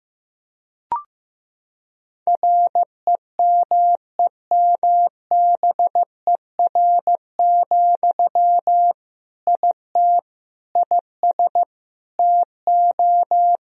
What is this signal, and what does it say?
Morse code, heard as a keyed tone, about 16 wpm, 700 Hz. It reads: REMEMBER, IT IS TO
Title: Sentences from Top 1000 Words - Review Part 2 - 15wpm